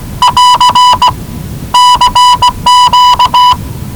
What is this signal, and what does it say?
Morse code, heard as a keyed tone, about 19 wpm, 990 Hz. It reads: <AR> CQ
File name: WellingtonRadioZlwCloseDownOnHfAndMf / 01-Zlw 4277khz 417.5 qrt until 0925z